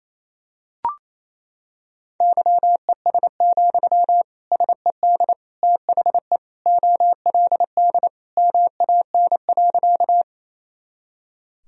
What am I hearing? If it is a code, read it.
Morse code, heard as a keyed tone, about 28 wpm, 700 Hz. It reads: YES, SED THE OLD MAN.
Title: Sentences from Top 500 Words and Abbreviations 28wpm